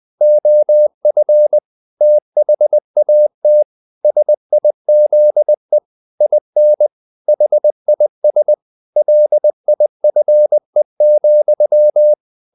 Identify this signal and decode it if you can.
Morse code, heard as a keyed tone, about 20 wpm, 600 Hz. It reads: OF THAT SIZE IN HIS LIFE,
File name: the old man and the sea / the old man and the sea part 1 20wpm